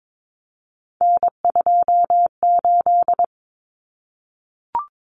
Morse code, heard as a keyed tone, about 22 wpm, 700 Hz. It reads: N28
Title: Common Cut Numbers - 3-Digit 22wpm